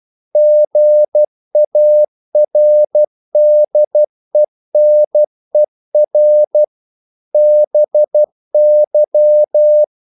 Morse code, heard as a keyed tone, about 12 wpm, 600 Hz. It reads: GARDENER BY
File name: mp3 files / the gardener - 12wpm